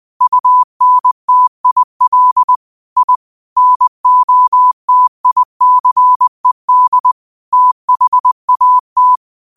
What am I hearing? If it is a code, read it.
Morse code, heard as a keyed tone, about 20 wpm, 1000 Hz. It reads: UNTIL I NOTICED THAT